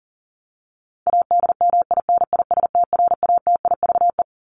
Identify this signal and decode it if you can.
Morse code, heard as a keyed tone, about 40 wpm, 700 Hz. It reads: ADMINISTRATIVE
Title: Top 1501-1750 Words 40wpm